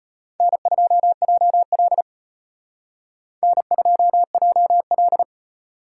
Morse code, heard as a keyed tone, about 31 wpm, 700 Hz. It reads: N2JL N2JL
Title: Speed Racing - USA 1x2 Call Signs 25wpm